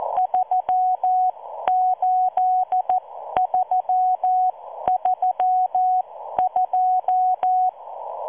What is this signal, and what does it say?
Morse code, heard as a keyed tone, about 14 wpm, spaced as for 12 wpm, 740 Hz. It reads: 38332